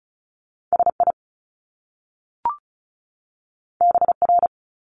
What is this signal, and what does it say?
Morse code, heard as a keyed tone, about 36 wpm, 700 Hz. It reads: SI BR
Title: Top Two Letter Combinations in Words 35wpm